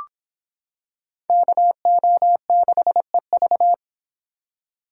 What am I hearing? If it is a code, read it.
Morse code, heard as a keyed tone, about 26 wpm, 700 Hz. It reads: KO6EV